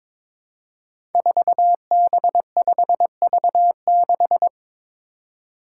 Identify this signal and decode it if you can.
Morse code, heard as a keyed tone, about 22 wpm, 700 Hz. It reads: 4B5V6
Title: Five Challenging Letter-Number Combinations 22wpm